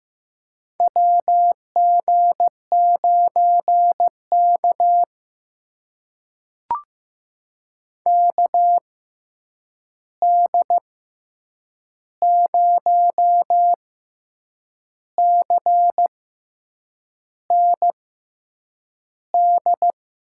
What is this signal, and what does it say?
Morse code, heard as a keyed tone, about 15 wpm, spaced as for 7 wpm, 700 Hz. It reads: WG9K KD0CND
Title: USA Call Signs - 6x Character Spacing - 15wpm